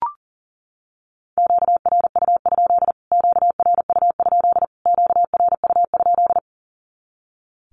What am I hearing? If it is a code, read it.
Morse code, heard as a keyed tone, about 40 wpm, 700 Hz. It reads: QRU? QRU? QRU?
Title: Q-Codes Repeated 3x 40wpm